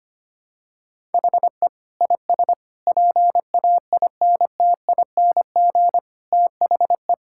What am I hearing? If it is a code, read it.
Morse code, heard as a keyed tone, about 25 wpm, 700 Hz. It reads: HE IS PAINTING THE
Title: Sentences from Top 1000 Words - Review Part 6 - 25wpm